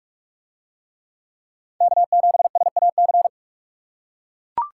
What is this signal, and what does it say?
Morse code, heard as a keyed tone, about 41 wpm, 700 Hz. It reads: K7SAC